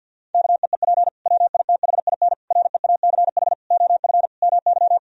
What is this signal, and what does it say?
Morse code, heard as a keyed tone, about 50 wpm, 700 Hz. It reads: KEEP WITHIN REACH OF MY
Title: Sentences from Top 1000 Words - Review Part 4 - 50wpm